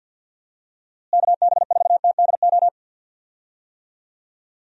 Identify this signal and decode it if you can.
Morse code, heard as a keyed tone, about 49 wpm, 700 Hz. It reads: KB4TDO
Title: USA Call Signs - 6 Characters - Repeated 3x - 50wpm